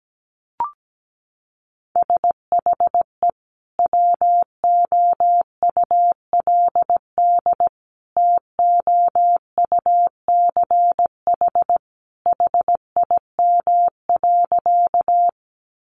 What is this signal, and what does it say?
Morse code, heard as a keyed tone, about 17 wpm, 700 Hz. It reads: SHE WOULD TOUCH HIM.
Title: Sentences from Top 1200 Words 17wpm